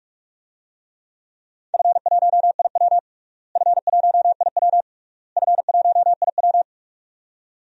Morse code, heard as a keyed tone, about 41 wpm, 700 Hz. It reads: F1IW F1IW F1IW
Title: DXCC-100 Call Signs - 4 Characters - Repeated 3x - 45wpm